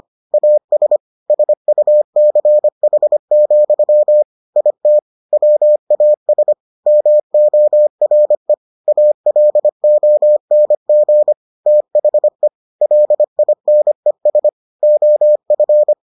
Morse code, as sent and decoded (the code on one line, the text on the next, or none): .- ... / ... ..- -.-. .... --..-- / .. - / .-- .- ... / -- --- .-. / .- .-.. --- -. --. / - .... / .-.. .. -. ... / --- ..-.
AS SUCH, IT WAS MORE ALONG THE LINES OF